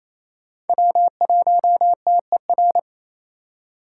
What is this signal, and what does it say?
Morse code, heard as a keyed tone, about 27 wpm, 700 Hz. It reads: W1TER
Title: K1USN SST QSO Practice - 28wpm